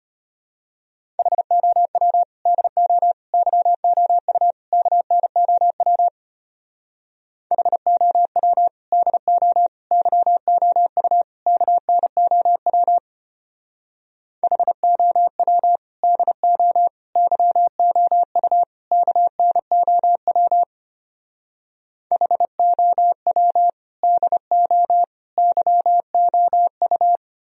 Morse code, heard as a keyed tone, about 30 wpm, 700 Hz. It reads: HOW DO YOU KNOW HOW DO YOU KNOW HOW DO YOU KNOW HOW DO YOU
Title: Speed-Racing - Sets of 4 Words 25wpm